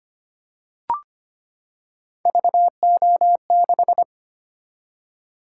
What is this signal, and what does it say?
Morse code, heard as a keyed tone, about 25 wpm, 700 Hz. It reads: VO6
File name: Three Letter-Number Combinations - 25wpm